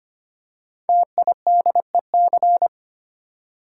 Morse code, heard as a keyed tone, about 25 wpm, 700 Hz. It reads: TIDEC
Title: Code Groups - 5 Letters - 25wpm